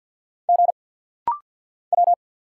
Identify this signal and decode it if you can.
Morse code, heard as a keyed tone, about 50 wpm, 700 Hz. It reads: G W
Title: Single Letters - Rapid-Fire 50wpm